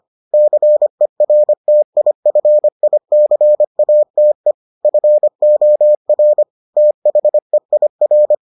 Morse code, heard as a keyed tone, about 25 wpm, 600 Hz. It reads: CERTIFICATE FOR THEIR